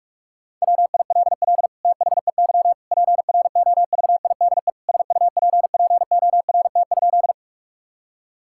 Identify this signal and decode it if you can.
Morse code, heard as a keyed tone, about 41 wpm, 700 Hz. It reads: WILL THEY PROVIDE SUPPORT?
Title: Sentences from Top 1000 Words - Review Part 5 - 45wpm